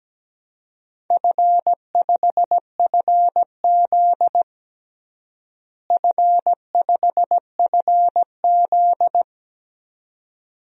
Morse code, heard as a keyed tone, about 17 wpm, 700 Hz. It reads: F5FZ F5FZ
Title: DXCC-100 Call Signs - 4 Characters - Repeated 2x - 17wpm